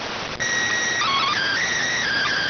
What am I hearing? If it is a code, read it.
SSTV (slow-scan television), PD 290.